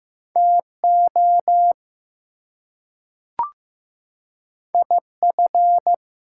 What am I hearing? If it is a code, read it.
Morse code, heard as a keyed tone, about 15 wpm, 700 Hz. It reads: TO IF
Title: Two Letter Words - 15wpm